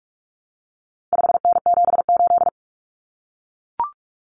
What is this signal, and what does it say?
Morse code, heard as a keyed tone, about 44 wpm, 700 Hz. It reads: 5N78
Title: Common Cut Numbers - 4-Digit 45wpm